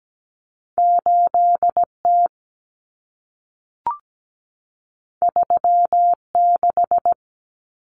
Morse code, heard as a keyed tone, about 17 wpm, 700 Hz. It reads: 8T 36
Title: Common Cut Numbers - 2-Digit 17wpm